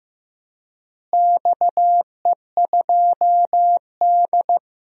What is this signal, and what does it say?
Morse code, heard as a keyed tone, about 16 wpm, 700 Hz. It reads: XE2D
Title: DXCC-100 Call Signs - 4 Characters - Repeated 3x - 15wpm